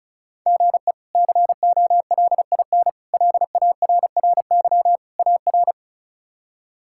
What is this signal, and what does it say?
Morse code, heard as a keyed tone, about 34 wpm, 700 Hz. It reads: GE COLIN LARRY AR
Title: K1USN SST QSO Practice - 35wpm